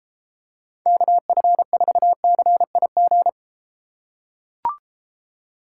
Morse code, heard as a keyed tone, about 32 wpm, 700 Hz. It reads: KF4CIG